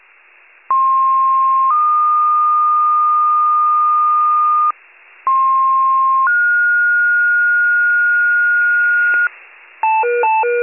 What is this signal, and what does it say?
Two-tone page, heard as a one-second tone, then a three-second tone, about 1060 then 1220 Hz, about 1060 then 1430 Hz, with noise. It is heard from 0.7 to 9.3 s.
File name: pager / 98 RESCUE / wcalert